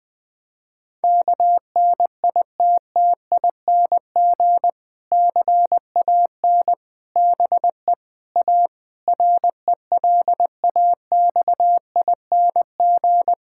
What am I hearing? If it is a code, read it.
Morse code, heard as a keyed tone, about 21 wpm, 700 Hz. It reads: KNITTING CAN BE A RELAXING
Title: Difficult Sentences with Hints - 20wpm